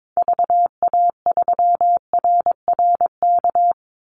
Morse code, heard as a keyed tone, about 22 wpm, 700 Hz. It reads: VA3RRK
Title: Call Signs Repeated 3x - 22wpm